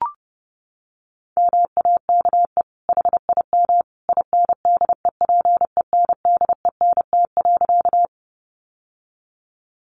Morse code, heard as a keyed tone, about 30 wpm, 700 Hz. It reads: MAKE HIM INDEPENDENT.
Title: Sentences from Top 900 Words 30wpm